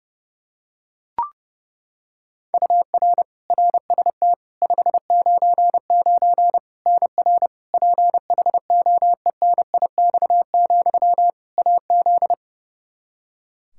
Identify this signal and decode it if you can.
Morse code, heard as a keyed tone, about 30 wpm, 700 Hz. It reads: UR RST 599 NR PHOENIX, AZ
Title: Practice QSO - Exchange 1 - Beginning - 30wpm